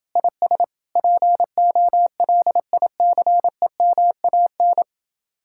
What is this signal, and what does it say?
Morse code, heard as a keyed tone, about 28 wpm, 700 Hz. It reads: IS POLICEMAN